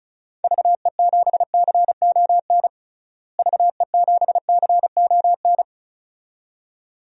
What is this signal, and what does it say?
Morse code, heard as a keyed tone, about 34 wpm, 700 Hz. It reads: VE7CON VE7CON
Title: DXCC-100 Call Signs - 6 Characters - Repeated 2x - 35wpm